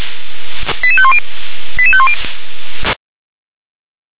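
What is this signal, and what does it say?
Five-tone selective call (ZVEI), heard as quick runs of five tones, tones of about 70 ms. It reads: 88510 88510
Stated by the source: Received on SDR; ZVEI five-tone selective call.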